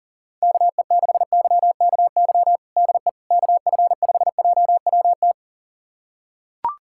Morse code, heard as a keyed tone, about 40 wpm, 700 Hz. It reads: KE6YKY DE KF5JWT
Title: USA Callsign DE Callsign - 6 Characters - 40wpm